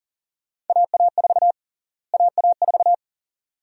Morse code, heard as a keyed tone, about 40 wpm, 700 Hz. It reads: AA4 AA4
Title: USA Call Sign Prefixes - Repeated 2x - 40wpm